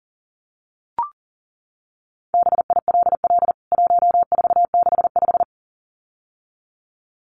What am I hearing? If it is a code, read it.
Morse code, heard as a keyed tone, about 40 wpm, 700 Hz. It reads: BILL 1465
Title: CWT Exchange 40wpm